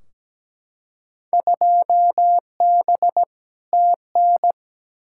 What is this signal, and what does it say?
Morse code, heard as a keyed tone, about 17 wpm, 700 Hz. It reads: 2B TN
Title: Field Day Exchanges Encore - 17wpm